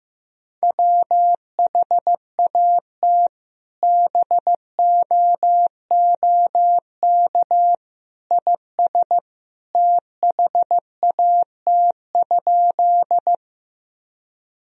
Morse code, heard as a keyed tone, about 15 wpm, 700 Hz. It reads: WHAT BOOK IS THAT?
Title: Sentences from Top 1000 Words - Review Part 1 - 15wpm